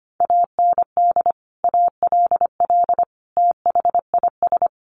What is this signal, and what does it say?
Morse code, heard as a keyed tone, about 25 wpm, 700 Hz. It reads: AND ALL THIS
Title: Sentences from Top 1200 Words 25wpm